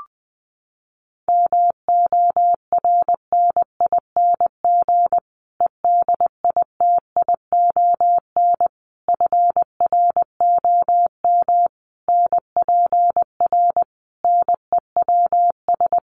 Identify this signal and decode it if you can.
Morse code, heard as a keyed tone, about 20 wpm, 700 Hz. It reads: MORNING EDITION FROM NPR NEWS
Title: Sets of 5 Words Encore 20wpm